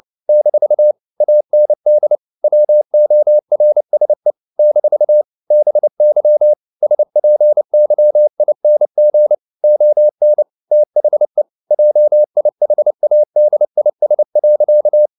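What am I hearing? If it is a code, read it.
Morse code, heard as a keyed tone, about 29 wpm, 600 Hz. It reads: - AND WORSE - BY SPYING ON THE JIHADIS.
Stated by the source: Morse code at 29 wpm, sending Ken Follett's novel Never in English, from CW med Gnister.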